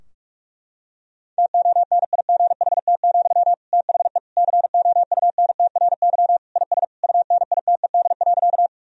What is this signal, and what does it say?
Morse code, heard as a keyed tone, about 42 wpm, 700 Hz. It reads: TONIGHT, THE COUNTRY IS UNITED.
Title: Sentences from Top 1000 Words - Review Part 4 - 45wpm